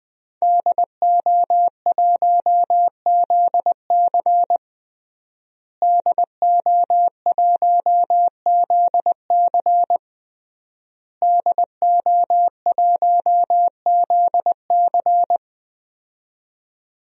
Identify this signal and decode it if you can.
Morse code, heard as a keyed tone, about 21 wpm, 700 Hz. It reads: DO1ZC DO1ZC DO1ZC
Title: DXCC-100 Call Signs - 5 Characters - Repeated 3x - 20wpm